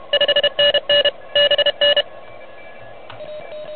Morse code, heard as a keyed tone, about 30 wpm, 2900 Hz, with noise. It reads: 5NN BN
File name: truSDX